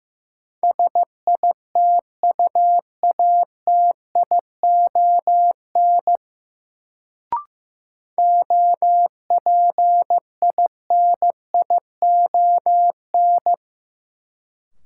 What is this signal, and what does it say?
Morse code, heard as a keyed tone, about 15 wpm, 700 Hz. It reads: SITUATION OPINION
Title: Sentences from Top 1200 Words - New Word Drill Alt - 15wpm